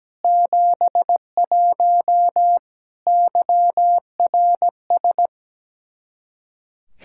Morse code, heard as a keyed tone, about 17 wpm, 700 Hz. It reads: 71 YRS
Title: K1USN SST QSO Practice 17wpm